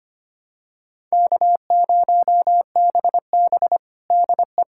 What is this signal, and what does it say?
Morse code, heard as a keyed tone, about 25 wpm, 700 Hz. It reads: K0BB DE
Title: USA Callsign DE Callsign - 4 Characters - 25wpm